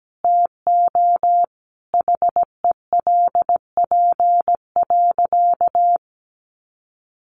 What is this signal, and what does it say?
Morse code, heard as a keyed tone, about 17 wpm, 700 Hz. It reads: TO HELP.
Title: Sentences from Top 300 Words 17wpm